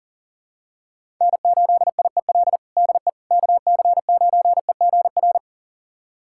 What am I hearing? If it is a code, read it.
Morse code, heard as a keyed tone, about 40 wpm, 700 Hz. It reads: N8IEL DE KC9EGR
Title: USA Callsign DE Callsign 40wpm